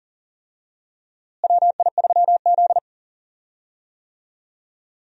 Morse code, heard as a keyed tone, about 38 wpm, 700 Hz. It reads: WI3Z
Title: USA Call Signs - 4 Characters - Repeated 3x - 40wpm